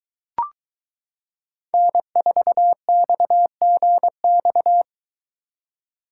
Morse code, heard as a keyed tone, about 23 wpm, 700 Hz. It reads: N4XGX